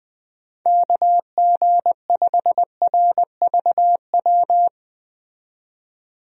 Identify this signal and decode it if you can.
Morse code, heard as a keyed tone, about 20 wpm, 700 Hz. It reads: KG5RVW
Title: USA Call Signs - 6 Characters - 20wpm